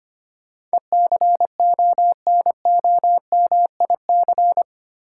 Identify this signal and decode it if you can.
Morse code, heard as a keyed tone, about 25 wpm, 700 Hz. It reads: ECONOMIC
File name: 8 Letter Words - First 3 Letters Given - 25wpm